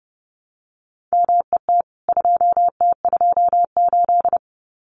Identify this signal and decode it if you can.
Morse code, heard as a keyed tone, about 30 wpm, 700 Hz. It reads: MET 2T28
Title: CWT Exchange 30wpm